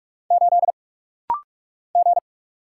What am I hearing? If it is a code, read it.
Morse code, heard as a keyed tone, about 46 wpm, 690 Hz. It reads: 8 G
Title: Single Letter-Number - Rapid-Fire 45wpm